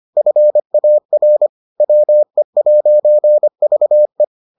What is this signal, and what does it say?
Morse code, heard as a keyed tone, about 25 wpm, 600 Hz. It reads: FAR WE'VE